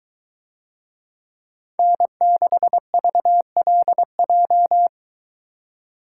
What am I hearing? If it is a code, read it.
Morse code, heard as a keyed tone, about 23 wpm, 700 Hz. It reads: N6VLJ